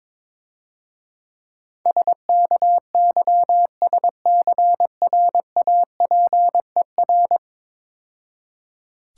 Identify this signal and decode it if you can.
Morse code, heard as a keyed tone, about 22 wpm, 700 Hz. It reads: SKYSCRAPER
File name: Easy Words - Speed-Racing with Hints - 22wpm